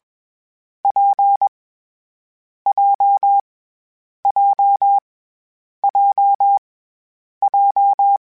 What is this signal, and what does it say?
Morse code, heard as a keyed tone, about 21 wpm, 800 Hz. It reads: P J J J J
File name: Variable Speed / Variable Speed-mp3s / WPJ WPJ1 sp @20